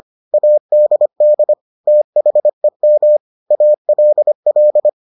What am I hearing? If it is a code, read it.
Morse code, heard as a keyed tone, about 25 wpm, 600 Hz. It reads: ADD THEM ALL